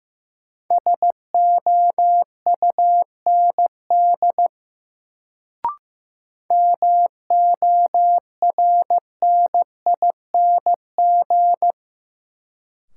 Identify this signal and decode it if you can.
Morse code, heard as a keyed tone, about 15 wpm, 700 Hz. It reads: SOUND MORNING
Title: Sets of 3 Words - New Word Drill - 15wpm